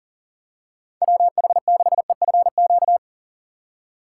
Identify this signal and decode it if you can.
Morse code, heard as a keyed tone, about 40 wpm, 700 Hz. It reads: WH6EFQ
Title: USA Call Signs - 6 Characters - 3x Character Spacing - 40wpm